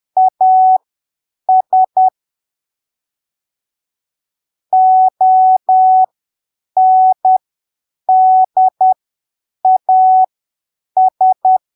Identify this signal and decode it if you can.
Morse code, heard as a keyed tone, about 10 wpm, spaced as for 7 wpm, 750 Hz. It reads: AS ONDAS
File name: arquivos / texto / texto 13 10ppm